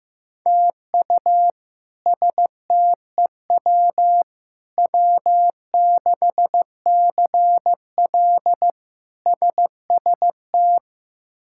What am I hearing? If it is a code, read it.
Morse code, heard as a keyed tone, about 15 wpm, 700 Hz. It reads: TU STEW W6CL SST
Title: K1USN SST QSO Practice - 15wpm